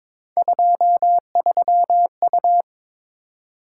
Morse code, heard as a keyed tone, about 22 wpm, 700 Hz. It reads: 23U
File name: practice / 22wpm / Three Letter-Number Combinations - 22wpm